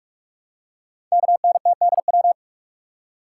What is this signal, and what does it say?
Morse code, heard as a keyed tone, about 44 wpm, 700 Hz. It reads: KNTDW
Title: Code Groups - 5 Letters - 45wpm